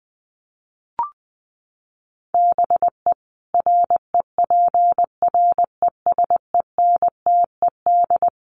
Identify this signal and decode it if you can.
Morse code, heard as a keyed tone, about 20 wpm, 700 Hz. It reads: BE REPRESENTED
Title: Sets of 5 Words 20wpm